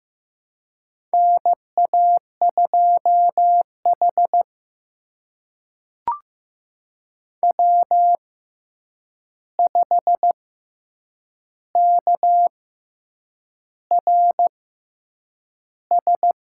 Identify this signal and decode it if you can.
Morse code, heard as a keyed tone, about 15 wpm, spaced as for 7 wpm, 700 Hz. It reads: NA2H W5KRS
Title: USA Call Signs - 6x Character Spacing - 15wpm